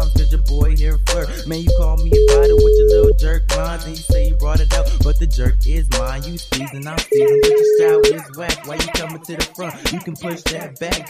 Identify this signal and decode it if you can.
Ringback tone (Europe, 425 Hz), with speech.